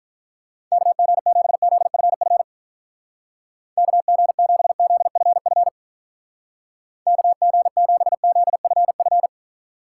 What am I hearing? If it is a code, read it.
Morse code, heard as a keyed tone, about 44 wpm, 700 Hz. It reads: KG7ZFF KG7ZFF KG7ZFF